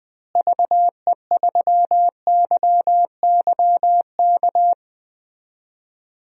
Morse code, heard as a keyed tone, about 21 wpm, 700 Hz. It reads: VE3YYK
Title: DXCC-100 Call Signs - 6 Characters - Repeated 3x - 20wpm